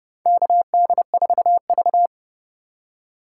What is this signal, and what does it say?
Morse code, heard as a keyed tone, about 29 wpm, 700 Hz. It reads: KD4V
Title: USA Call Signs - 4 Characters - 2x Character Spacing - 30wpm